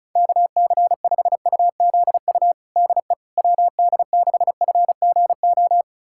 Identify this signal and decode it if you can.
Morse code, heard as a keyed tone, about 36 wpm, 690 Hz. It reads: KC5UZU DE WD6FGO
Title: USA Callsign DE Callsign 35wpm